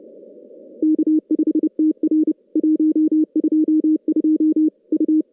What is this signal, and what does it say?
Morse code, heard as a keyed tone, about 30 wpm, 310 Hz. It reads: K5TR 122 U
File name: contests / CW-SS-k5tr- -2300